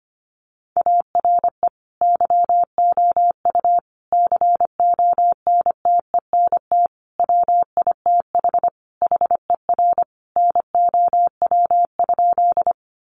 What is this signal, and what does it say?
Morse code, heard as a keyed tone, about 25 wpm, 700 Hz. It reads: ARE YOU CONTENT WITH HER NOW?
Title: Sentences from Top 900 Words 25wpm